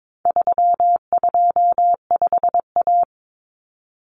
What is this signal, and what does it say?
Morse code, heard as a keyed tone, about 22 wpm, 700 Hz. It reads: 325A
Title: Common Cut Numbers - 4-Digit 22wpm